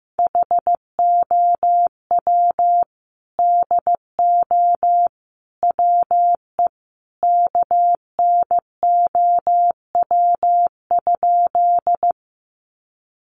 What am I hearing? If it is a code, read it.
Morse code, heard as a keyed tone, about 15 wpm, 700 Hz. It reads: HOW DO WE KNOW?
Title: A Sound Of Thunder 15wpm